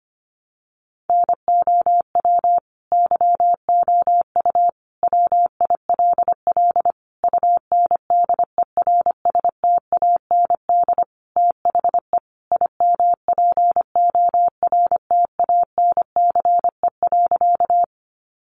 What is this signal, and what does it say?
Morse code, heard as a keyed tone, about 25 wpm, 700 Hz. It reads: NOW YOU WILL UNDERSTAND THE IMPORTANCE.
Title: Sentences from Top 1200 Words 25wpm